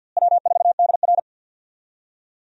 Morse code, heard as a keyed tone, about 48 wpm, 700 Hz. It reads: W4DR
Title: USA Call Signs - 4 Characters - Repeated 2x - 50wpm